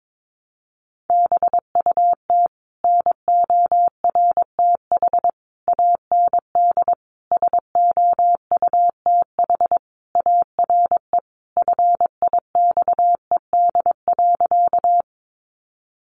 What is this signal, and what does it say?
Morse code, heard as a keyed tone, about 22 wpm, 700 Hz. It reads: BUT NORTH AND SOUTH ARE FIXED.
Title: Sentences from Top 1100 Words 22wpm